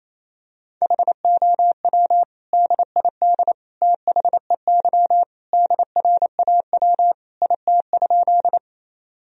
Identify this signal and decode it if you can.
Morse code, heard as a keyed tone, about 28 wpm, 700 Hz. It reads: HOW DID THEY DRAW IT?
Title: Sentences from Top 1000 Words - Review Part 1 - 28wpm